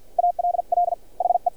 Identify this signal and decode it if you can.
Morse code, heard as a keyed tone, about 51 wpm, spaced as for 44 wpm, 700 Hz. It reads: ALL HE
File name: archived / CRSnet